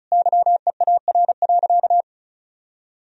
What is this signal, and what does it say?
Morse code, heard as a keyed tone, about 36 wpm, 700 Hz. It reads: YEAR.
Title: Sentences from Top 1000 Words - Review Part 1 - 35wpm